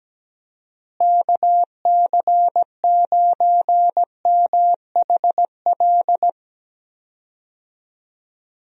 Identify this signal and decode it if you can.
Morse code, heard as a keyed tone, about 17 wpm, 700 Hz. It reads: KC9MHL
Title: USA Call Signs - 6 Characters - Repeated 2x - 17wpm